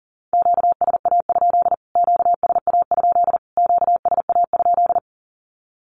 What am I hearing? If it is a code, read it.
Morse code, heard as a keyed tone, about 40 wpm, 700 Hz. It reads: QSA? QSA? QSA?
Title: Q-Codes Repeated 3x 40wpm